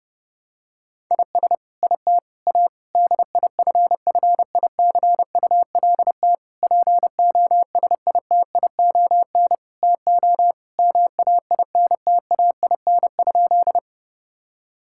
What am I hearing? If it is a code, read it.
Morse code, heard as a keyed tone, about 30 wpm, 700 Hz. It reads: IS IT A DIFFICULT POSITION TO MAINTAIN?
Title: Sentences from Top 1000 Words - Review Part 5 - 30wpm